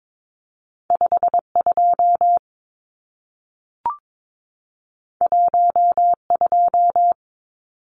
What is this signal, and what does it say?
Morse code, heard as a keyed tone, about 22 wpm, 700 Hz. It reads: 52 12
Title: Numbers - 2-Digit 22wpm